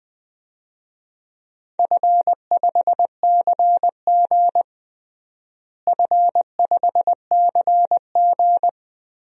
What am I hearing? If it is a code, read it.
Morse code, heard as a keyed tone, about 20 wpm, 700 Hz. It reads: F5CG F5CG
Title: DXCC-100 Call Signs - 4 Characters - Repeated 3x - 20wpm